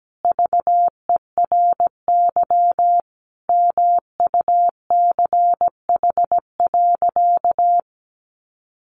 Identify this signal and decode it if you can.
Morse code, heard as a keyed tone, about 17 wpm, 700 Hz. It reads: VERY MUCH.
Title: Sentences from Top 600 Words 17wpm